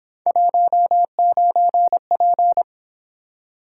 Morse code, heard as a keyed tone, about 26 wpm, 700 Hz. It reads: 19P
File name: Speed-Racing - Three Letter-Number Combinations - 22wpm